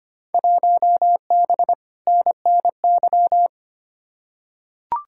Morse code, heard as a keyed tone, about 25 wpm, 700 Hz. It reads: 1B NNY
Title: Field Day Exchanges Encore - 25wpm